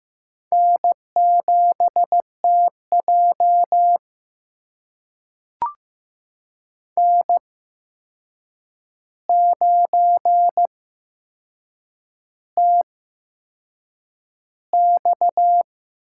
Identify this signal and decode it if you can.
Morse code, heard as a keyed tone, about 15 wpm, spaced as for 6 wpm, 700 Hz. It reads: N7TJ N9TX
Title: USA Call Signs - 4 Characters - 8x Character Spacing - 15wpm